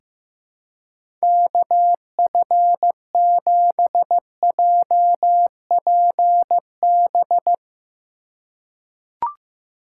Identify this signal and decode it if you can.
Morse code, heard as a keyed tone, about 15 wpm, 700 Hz. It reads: KF7JPB
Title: USA Call Signs - 2x Character Spacing - 15wpm